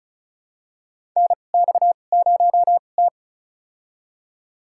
Morse code, heard as a keyed tone, about 34 wpm, spaced as for 25 wpm, 700 Hz. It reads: NX0T
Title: USA Call Signs - 4 Characters - 2x Character Spacing - 35wpm